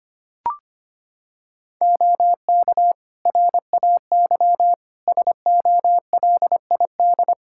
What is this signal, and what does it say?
Morse code, heard as a keyed tone, about 25 wpm, 700 Hz. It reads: OK RAY SOLID
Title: Practice QSO - Exchange 3 - More Info - 25wpm